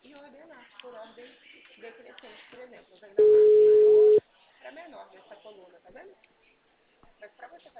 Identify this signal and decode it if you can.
Ringback tone (Europe, 425 Hz), with speech.
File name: out- -2007- - - .268